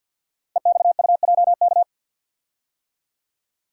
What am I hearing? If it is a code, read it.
Morse code, heard as a keyed tone, about 50 wpm, 700 Hz. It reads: EXUJK
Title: Code Groups - 5 Letters - 50wpm